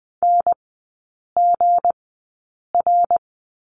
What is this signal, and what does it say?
Morse code, heard as a keyed tone, about 19 wpm, spaced as for 8 wpm, 700 Hz. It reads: NGR
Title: 212 Lesson 34e - Number 8 - Three Characters 20wpm